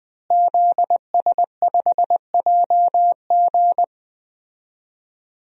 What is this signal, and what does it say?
Morse code, heard as a keyed tone, about 20 wpm, 700 Hz. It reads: ZS5JG
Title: DXCC-100 Call Signs - 5 Characters - Repeated 3x - 20wpm